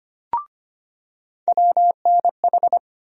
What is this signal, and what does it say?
Morse code, heard as a keyed tone, about 25 wpm, 700 Hz. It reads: WNH
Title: USA Call Sign Postfixes - 3 Letters 25wpm